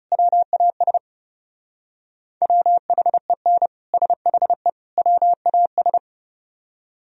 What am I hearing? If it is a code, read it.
Morse code, heard as a keyed tone, about 37 wpm, 700 Hz. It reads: WAS WHEN SHE WAS
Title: Speed-Racing - Sets of 3 Words 30wpm